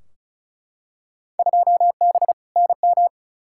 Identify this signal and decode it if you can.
Morse code, heard as a keyed tone, about 34 wpm, 700 Hz. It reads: 2B NM
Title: Field Day Exchanges Encore - 35wpm